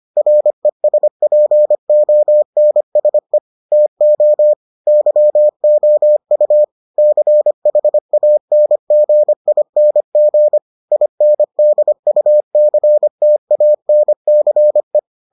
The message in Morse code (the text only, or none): RESPONSE TO YOU CHANGING INDUCTANCE